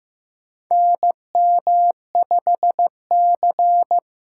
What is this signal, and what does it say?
Morse code, heard as a keyed tone, about 15 wpm, 700 Hz. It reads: NM5C
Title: USA Call Signs - 2x Character Spacing - 15wpm